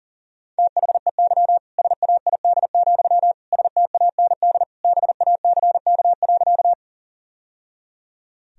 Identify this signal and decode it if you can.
Morse code, heard as a keyed tone, about 37 wpm, 700 Hz. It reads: THEY SAID, STAND BACK.
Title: Sentences from Top 1000 Words - Review Part 1 - 40wpm